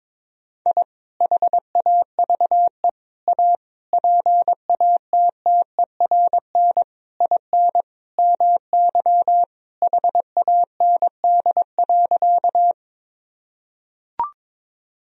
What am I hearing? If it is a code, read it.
Morse code, heard as a keyed tone, about 21 wpm, 700 Hz. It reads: I HAVE A PATTERN IN MY HAND.
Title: Sentences from Top 1000 Words - Review Part 2 - 22wpm